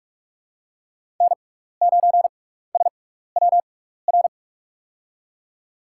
Morse code, heard as a keyed tone, about 44 wpm, spaced as for 15 wpm, 700 Hz. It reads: N9SWR
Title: USA Call Signs - 5 Characters - 6x Character Spacing - 45wpm